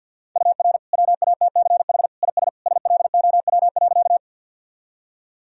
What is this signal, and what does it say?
Morse code, heard as a keyed tone, about 51 wpm, 700 Hz. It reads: UR WATCH IS SLOW.
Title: Sentences from Top 400 Words and Abbreviations 50wpm